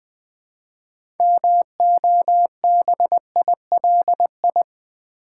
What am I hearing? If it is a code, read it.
Morse code, heard as a keyed tone, about 20 wpm, 700 Hz. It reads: MOBILI
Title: Build A Word - 3 Letters At A Time - 8-Letter Words - 20wpm